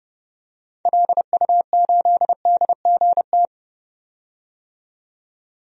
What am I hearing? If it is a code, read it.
Morse code, heard as a keyed tone, about 30 wpm, 700 Hz. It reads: LU8DGT
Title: DXCC-100 Call Signs - 6 Characters - 30wpm